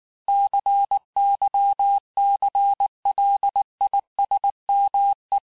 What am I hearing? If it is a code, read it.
Morse code, heard as a keyed tone, about 19 wpm, 800 Hz. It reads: CYCLISME
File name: cours-du-lundi files / cours-1589-texte-clair-vitesse-18mpm-puis-23mpm